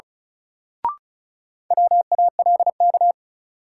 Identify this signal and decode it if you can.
Morse code, heard as a keyed tone, about 34 wpm, 700 Hz. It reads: WALK
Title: Sentences from Top 300 Words - New Word Drill - 35wpm